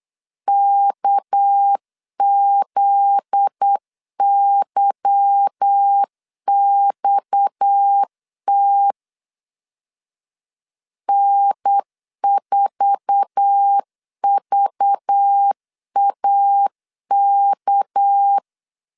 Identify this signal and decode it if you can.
Morse code, heard as a keyed tone, about 8 wpm, 800 Hz. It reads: KZYXT N4VAK
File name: LEZIONI / 8 YJ45 / 8 YJ4 8wpm 3246